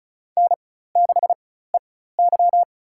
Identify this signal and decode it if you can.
Morse code, heard as a keyed tone, about 36 wpm, spaced as for 16 wpm, 700 Hz. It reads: N6EY
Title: USA Call Signs - 4x Character Spacing - 35wpm